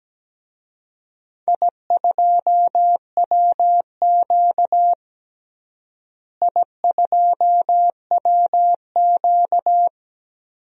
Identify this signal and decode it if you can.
Morse code, heard as a keyed tone, about 17 wpm, 700 Hz. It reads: I2WQ I2WQ
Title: DXCC-100 Call Signs - 4 Characters - Repeated 2x - 17wpm